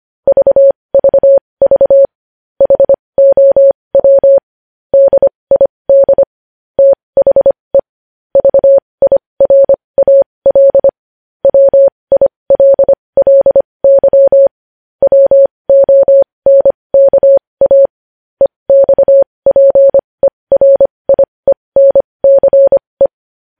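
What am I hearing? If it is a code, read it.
Morse code, heard as a keyed tone, about 25 wpm, spaced as for 20 wpm, 550 Hz. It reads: VVV HOW DID THE VIRAL WILLY WONKA EXPERIENCE
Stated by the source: Morse code news headlines sent at 20.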